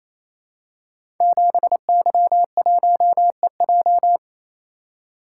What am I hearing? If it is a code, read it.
Morse code, heard as a keyed tone, about 29 wpm, 700 Hz. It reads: 7Y1EJ
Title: Code Groups - 5 Letter:Numbers - 28wpm